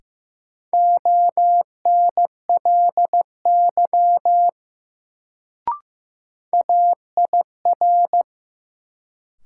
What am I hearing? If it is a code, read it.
Morse code, heard as a keyed tone, about 15 wpm, 700 Hz. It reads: ONLY AIR
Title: Sentences from Top 200 Words - New Word Drill - 15wpm